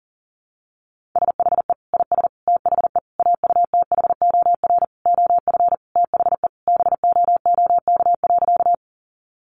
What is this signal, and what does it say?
Morse code, heard as a keyed tone, about 40 wpm, 700 Hz. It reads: SHE IS THE AUTHOR OF THE BOOK.
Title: Sentences from Top 1200 Words 40wpm